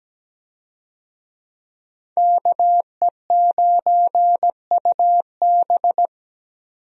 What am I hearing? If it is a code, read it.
Morse code, heard as a keyed tone, about 17 wpm, 700 Hz. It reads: KE9UB
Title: USA Call Signs - 5 Characters - 17wpm